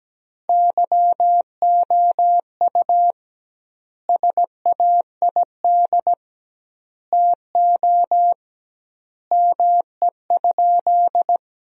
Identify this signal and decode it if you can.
Morse code, heard as a keyed tone, about 17 wpm, 700 Hz. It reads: YOU SAID TO ME?
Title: 2x Word-Spacing - Sentences from Top 200 Words - 17wpm